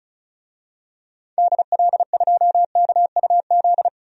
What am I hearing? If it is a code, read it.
Morse code, heard as a keyed tone, about 36 wpm, 700 Hz. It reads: DL2KUZ
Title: DXCC-100 Call Signs - 6 Characters - 35wpm